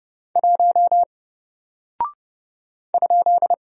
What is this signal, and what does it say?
Morse code, heard as a keyed tone, about 31 wpm, 700 Hz. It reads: 1 ?
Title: Single Character (Letter:Number:Punctuation:Prosign) 30wpm